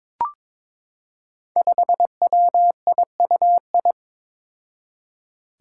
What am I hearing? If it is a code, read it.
Morse code, heard as a keyed tone, about 22 wpm, 700 Hz. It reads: 5WIUI
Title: Five Challenging Letter-Number Combinations 22wpm